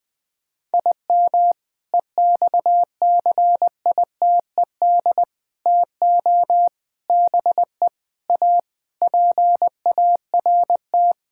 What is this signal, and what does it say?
Morse code, heard as a keyed tone, about 20 wpm, 700 Hz. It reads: IM EXCITED TO BE A PART
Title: Sentences with 5-Word Phrases - 20wpm